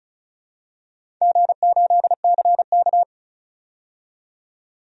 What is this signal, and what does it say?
Morse code, heard as a keyed tone, about 36 wpm, 700 Hz. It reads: G8CK
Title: DXCC-100 Call Signs - 4 Characters - 35wpm